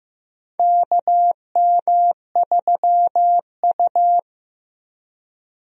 Morse code, heard as a keyed tone, about 15 wpm, 700 Hz. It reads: KM3U
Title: USA Call Signs - 4 Characters - Repeated 2x - 15wpm